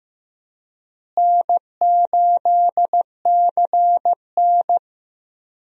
Morse code, heard as a keyed tone, about 15 wpm, 700 Hz. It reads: N8CN
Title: USA Call Signs - 4 Characters - Repeated 2x - 15wpm